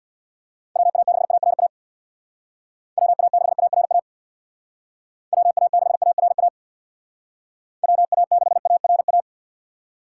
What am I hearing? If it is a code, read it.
Morse code, heard as a keyed tone, about 62 wpm, 700 Hz. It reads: WA6ARA WA6ARA WA6ARA WA6ARA
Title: Speed Racing - USA 2x3 Call Signs 50wpm